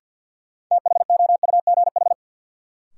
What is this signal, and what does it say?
Morse code, heard as a keyed tone, about 49 wpm, 700 Hz. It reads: THOUGH
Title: Sentences from Top 400 Words - New Word Drill - 50wpm